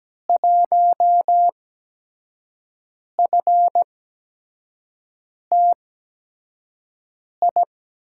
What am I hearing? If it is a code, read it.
Morse code, heard as a keyed tone, about 17 wpm, spaced as for 5 wpm, 700 Hz. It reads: 1FTI
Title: USA Call Signs - 6 Characters - 8x Character Spacing - 17wpm